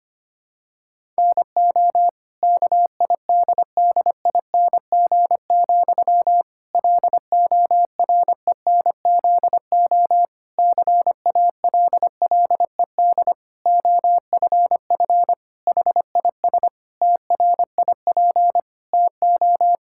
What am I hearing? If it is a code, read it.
Morse code, heard as a keyed tone, about 25 wpm, 700 Hz. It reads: NO KIDDING, LORENZO CALLED OFF HIS TRIP TO
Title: Pangram Challenge v2 - 25wpm